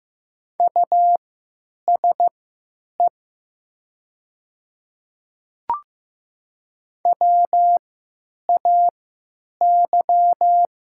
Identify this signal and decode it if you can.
Morse code, heard as a keyed tone, about 15 wpm, spaced as for 9 wpm, 700 Hz. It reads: USE WAY
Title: 3x Character Spacing - Top 100 Words - 15wpm